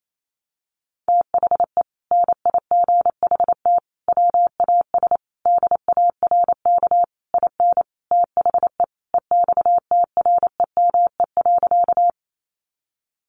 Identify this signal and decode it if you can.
Morse code, heard as a keyed tone, about 28 wpm, 700 Hz. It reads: THE NIGHT WAS DARK IN THE EXTREME.
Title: Sentences from Top 1300 Words 28wpm